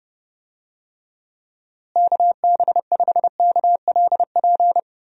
Morse code, heard as a keyed tone, about 32 wpm, 700 Hz. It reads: KB5KLP
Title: USA Call Signs - 6 Characters - Repeated 2x - 30wpm